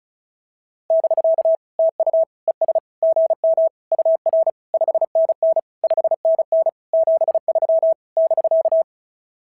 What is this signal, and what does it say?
Morse code, heard as a keyed tone, about 36 wpm, 650 Hz. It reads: <BK> TU ES GM UR 5NN 5NN 73 <BK>
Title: SOTA QSO Practice 35wpm